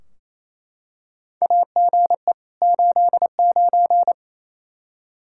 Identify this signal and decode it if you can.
Morse code, heard as a keyed tone, about 28 wpm, 700 Hz. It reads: AGE 89
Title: Practice QSO - Exchange 3 - More Info - 28wpm